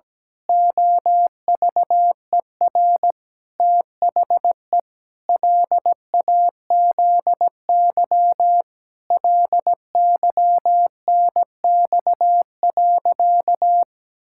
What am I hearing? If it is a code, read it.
Morse code, heard as a keyed tone, about 17 wpm, 700 Hz. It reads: OVER THE LAZY LYNX.
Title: Pangram Challenge v2 - 17wpm